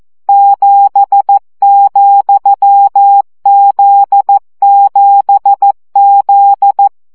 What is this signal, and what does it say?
Morse code, heard as a keyed tone, about 14 wpm, 800 Hz. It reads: 7,Z7Z
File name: CW gyakorlas z 7 vesszo 12 baud